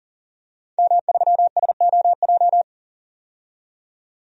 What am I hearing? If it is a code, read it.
Morse code, heard as a keyed tone, about 40 wpm, 700 Hz. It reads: M3SOJ
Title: DXCC-100 Call Signs - 5 Characters - 40wpm